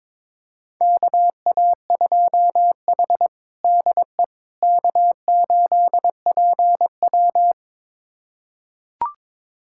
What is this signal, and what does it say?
Morse code, heard as a keyed tone, about 22 wpm, 700 Hz. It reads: KA2H DE K8PW
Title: USA Callsign DE Callsign - 4 Characters - 22wpm